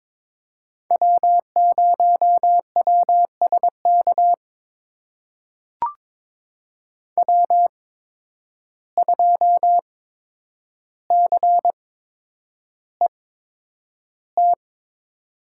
Morse code, heard as a keyed tone, about 23 wpm, spaced as for 10 wpm, 700 Hz. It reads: W0WSK W2CET
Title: USA Call Signs - 5 Characters - 8x Character Spacing - 22wpm